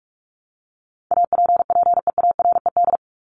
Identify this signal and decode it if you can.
Morse code, heard as a keyed tone, about 44 wpm, 700 Hz. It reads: APPEARED
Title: ED Suffix - Top 100 Words with Base Words 45wpm